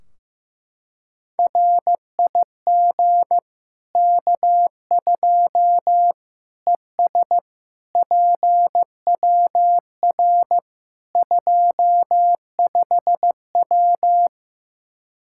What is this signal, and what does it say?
Morse code, heard as a keyed tone, about 15 wpm, 700 Hz. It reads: RIG K2 ES PWR 25W
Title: Practice QSO - Exchange 2 - Info - 15wpm